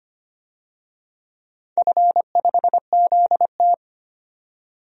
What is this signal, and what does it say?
Morse code, heard as a keyed tone, about 25 wpm, 700 Hz. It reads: F5ZT
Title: DXCC-100 Call Signs - 4 Characters - 25wpm